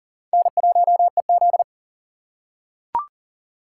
Morse code, heard as a keyed tone, about 40 wpm, 700 Hz. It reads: N1EZ
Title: USA Call Signs - 4 Characters - 3x Character Spacing - 40wpm